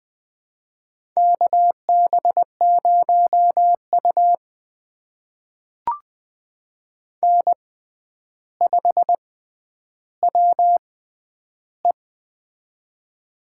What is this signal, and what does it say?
Morse code, heard as a keyed tone, about 20 wpm, spaced as for 10 wpm, 700 Hz. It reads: KB0U N5WE
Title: USA Call Signs - 4 Characters - 6x Character Spacing - 20wpm